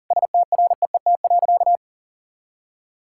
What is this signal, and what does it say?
Morse code, heard as a keyed tone, about 40 wpm, 700 Hz. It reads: STREET.
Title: Sentences from Top 1000 Words - Review Part 5 - 40wpm